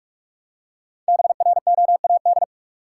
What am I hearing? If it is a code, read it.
Morse code, heard as a keyed tone, about 46 wpm, 700 Hz. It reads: BROAD
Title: Five Letter Words - 45wpm